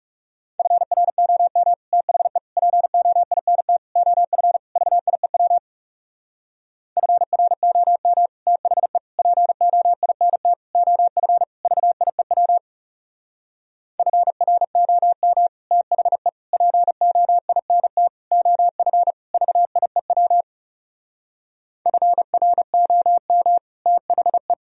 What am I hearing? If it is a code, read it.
Morse code, heard as a keyed tone, about 37 wpm, 700 Hz. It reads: FROM THE POINT OF VIEW FROM THE POINT OF VIEW FROM THE POINT OF VIEW FROM THE
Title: Speed-Racing - Sets of 5 Words 30wpm